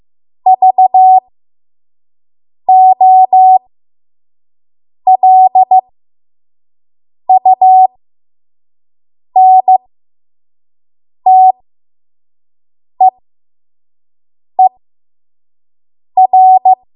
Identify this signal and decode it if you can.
Morse code, heard as a keyed tone, about 15 wpm, spaced as for 5 wpm, 750 Hz. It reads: VOLUNTEER
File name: Morse / 05WPM